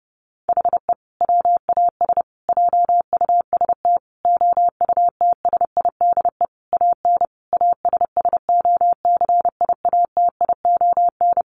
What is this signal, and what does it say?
Morse code, heard as a keyed tone, about 30 wpm, 700 Hz. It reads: HE WAS JUST OUTSIDE AN ASSOCIATION
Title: Sentences from Top 1100 Words 30wpm